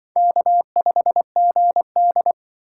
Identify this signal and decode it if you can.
Morse code, heard as a keyed tone, about 24 wpm, 700 Hz. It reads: K5GD
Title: Speed Racing - USA 1x2 Call Signs 20wpm